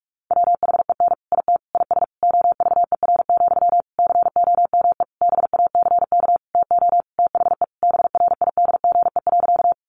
Morse code, heard as a keyed tone, about 44 wpm, 700 Hz. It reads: WHEN IT IS OVER, COME BACK TO THE BRIDGE.
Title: Sentences from Top 1200 Words 45wpm